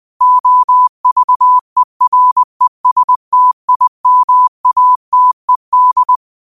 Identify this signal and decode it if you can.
Morse code, heard as a keyed tone, about 20 wpm, 1000 Hz. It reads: OVERESTIMATED
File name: qotdfiles / 20wpm